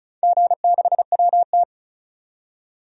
Morse code, heard as a keyed tone, about 36 wpm, 700 Hz. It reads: G6WT
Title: DXCC-100 Call Signs - 4 Characters - 35wpm